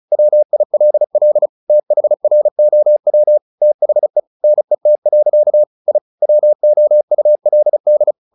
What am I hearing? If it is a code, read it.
Morse code, heard as a keyed tone, about 34 wpm, 600 Hz. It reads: WILL THROW THE NET. I WOULD
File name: the old man and the sea / the old man and the sea 1 35wpm